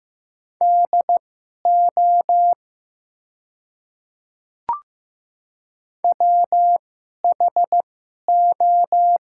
Morse code, heard as a keyed tone, about 15 wpm, spaced as for 11 wpm, 700 Hz. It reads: DO WHO
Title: 2x Character Spacing - Top 100 Words - 15wpm